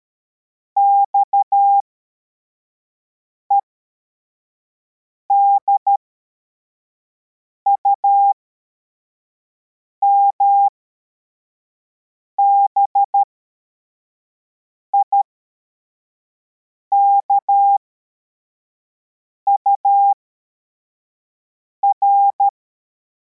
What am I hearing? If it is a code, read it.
Morse code, heard as a keyed tone, about 13 wpm, spaced as for 4 wpm, 800 Hz. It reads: XEDUM BIKUR
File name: formation / cw / Text42